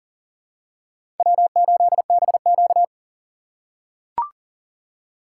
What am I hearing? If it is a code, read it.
Morse code, heard as a keyed tone, about 37 wpm, 700 Hz. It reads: W8BQ
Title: USA Call Signs - 4 Characters - 4x Character Spacing - 40wpm